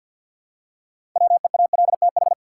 Morse code, heard as a keyed tone, about 50 wpm, 700 Hz. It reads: WEALTH
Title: Six Letter Words - 50wpm